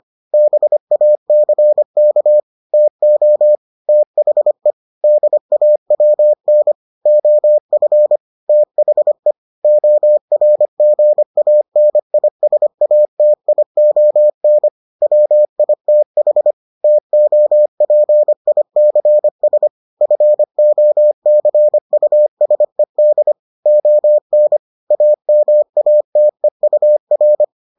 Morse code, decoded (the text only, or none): BACK TO THE DAWN OF THE ORGANISATION WITH TOPICS FOCUSED ON AMATEUR